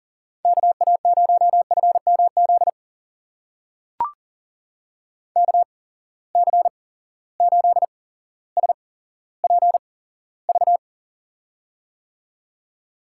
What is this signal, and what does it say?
Morse code, heard as a keyed tone, about 40 wpm, spaced as for 17 wpm, 700 Hz. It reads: KA0FMZ KC8SPV
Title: USA Call Signs - 6 Characters - 8x Character Spacing - 40wpm